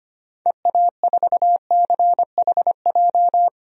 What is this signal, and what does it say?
Morse code, heard as a keyed tone, about 25 wpm, 690 Hz. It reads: EA4CHJ
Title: DXCC-100 Call Signs - 6 Characters - 25wpm